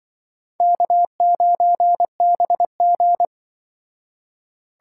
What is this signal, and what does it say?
Morse code, heard as a keyed tone, about 24 wpm, 700 Hz. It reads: K9BG
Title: Speed Racing - USA Call Signs 20wpm